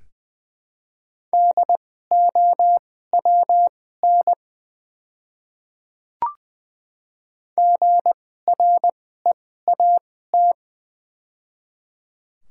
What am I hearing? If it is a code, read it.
Morse code, heard as a keyed tone, about 20 wpm, spaced as for 14 wpm, 700 Hz. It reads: DOWN GREAT
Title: 2x Character Spacing - Top 500 Words - 20wpm